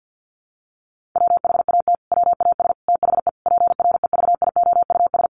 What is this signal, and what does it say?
Morse code, heard as a keyed tone, about 50 wpm, 700 Hz. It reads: WHAT WAS THE PREVIOUS
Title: Sentences from Top 1500 Words 50wpm